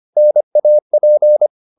Morse code, heard as a keyed tone, about 25 wpm, 600 Hz. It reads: NAP